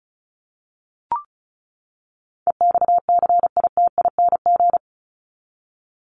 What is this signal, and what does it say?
Morse code, heard as a keyed tone, about 34 wpm, 700 Hz. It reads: EXCITING
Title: Top 1751-2000 Words 35wpm